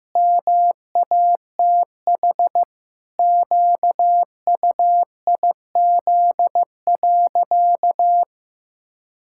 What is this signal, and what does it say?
Morse code, heard as a keyed tone, about 14 wpm, 700 Hz. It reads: MATH QUIZ.
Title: Pangram Challenge v2 - 15wpm